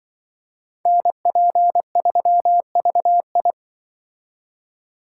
Morse code, heard as a keyed tone, about 24 wpm, 700 Hz. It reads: NP3VI
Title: Speed Racing - USA 2x2 Call Signs 20wpm